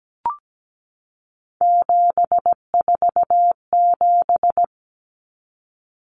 Morse code, heard as a keyed tone, about 17 wpm, 700 Hz. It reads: 747
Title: Common Cut Numbers - 3-Digit 17wpm